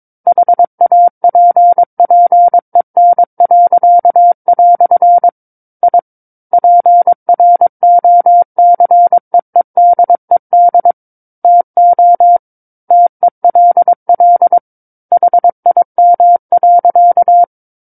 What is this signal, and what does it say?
Morse code, heard as a keyed tone, about 22 wpm, 700 Hz. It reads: HAPPEN." I PROCEEDED TO TELL HIM.
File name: Book One / 13-Chapter 13 - 22 WPM